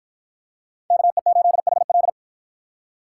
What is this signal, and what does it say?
Morse code, heard as a keyed tone, about 53 wpm, 700 Hz. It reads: KE8HL